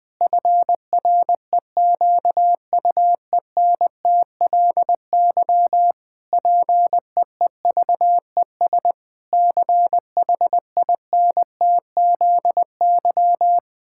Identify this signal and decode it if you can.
Morse code, heard as a keyed tone, about 20 wpm, 700 Hz. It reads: FREQUENTLY PEEVES CHINTZY
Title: Pangram Challenge v2 - 20wpm